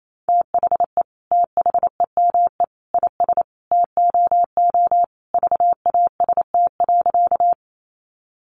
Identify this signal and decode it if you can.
Morse code, heard as a keyed tone, about 28 wpm, 700 Hz. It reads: THE THEME IS TOO VAST.
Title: Sentences from Top 1300 Words 28wpm